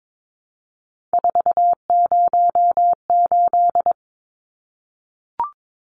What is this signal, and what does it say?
Morse code, heard as a keyed tone, about 22 wpm, 700 Hz. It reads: 408
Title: Numbers - 3-Digit 22wpm